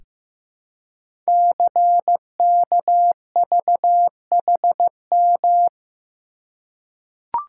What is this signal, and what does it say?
Morse code, heard as a keyed tone, about 15 wpm, 700 Hz. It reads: CKVHM
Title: Code Groups - 5 Letters - 15wpm